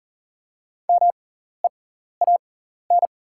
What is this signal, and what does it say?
Morse code, heard as a keyed tone, about 40 wpm, spaced as for 14 wpm, 700 Hz. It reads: MEAN
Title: 6x Character Spacing - Top 500 Words - 40wpm